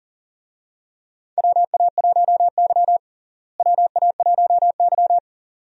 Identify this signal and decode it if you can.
Morse code, heard as a keyed tone, about 38 wpm, 700 Hz. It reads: WA1Y WA1Y
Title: USA Call Signs - 4 Characters - Repeated 2x - 40wpm